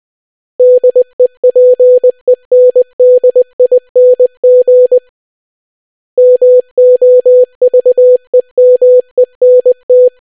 Morse code, heard as a keyed tone, about 20 wpm, 500 Hz. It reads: DEPENDING MOVEMENT